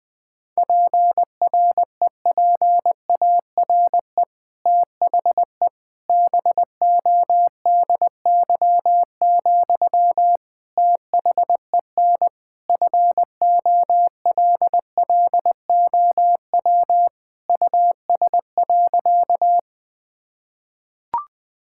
Morse code, heard as a keyed tone, about 20 wpm, 700 Hz. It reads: PREPARE THE BODY, THEN FOLLOW US.
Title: Sentences from Top 1000 Words - Review Part 6 - 20wpm